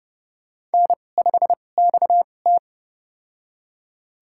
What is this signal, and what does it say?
Morse code, heard as a keyed tone, about 31 wpm, spaced as for 22 wpm, 700 Hz. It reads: N5XT
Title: USA Call Signs - 2x Character Spacing - 30wpm